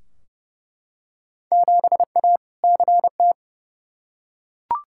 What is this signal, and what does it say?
Morse code, heard as a keyed tone, about 30 wpm, 700 Hz. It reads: 7A CT
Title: Field Day Exchanges - 30wpm